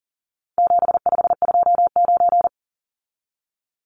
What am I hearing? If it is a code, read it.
Morse code, heard as a keyed tone, about 40 wpm, 700 Hz. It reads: 7529
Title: Numbers - 4-Digit 40wpm